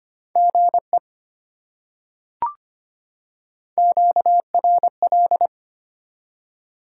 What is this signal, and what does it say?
Morse code, heard as a keyed tone, about 25 wpm, 700 Hz. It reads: GE QRL
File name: Top 20 QSO Elements - 25wpm